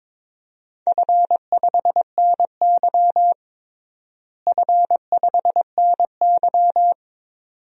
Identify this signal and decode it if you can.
Morse code, heard as a keyed tone, about 22 wpm, 700 Hz. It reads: F5NY F5NY
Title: DXCC-100 Call Signs - 4 Characters - Repeated 2x - 22wpm